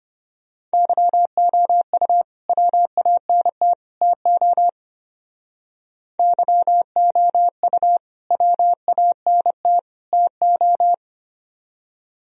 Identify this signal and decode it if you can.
Morse code, heard as a keyed tone, about 32 wpm, 700 Hz. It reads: YOU WANT TO YOU WANT TO